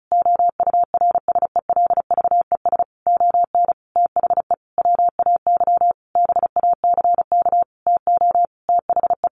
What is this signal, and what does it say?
Morse code, heard as a keyed tone, about 34 wpm, 700 Hz. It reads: OURSELVES ON THE WAY BACK TO THE
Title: A Sound Of Thunder 35wpm